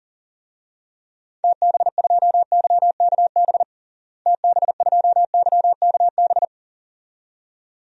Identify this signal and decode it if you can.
Morse code, heard as a keyed tone, about 37 wpm, 700 Hz. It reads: TB2YKB TB2YKB
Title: DXCC-100 Call Signs - 6 Characters - Repeated 2x - 40wpm